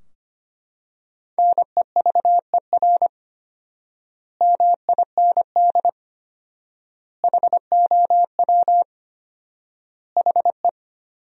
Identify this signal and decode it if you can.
Morse code, heard as a keyed tone, about 25 wpm, 700 Hz. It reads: NEVER MIND HOW HE
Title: 4x Word-Spacing - Sentences from Top 500 Words - 25wpm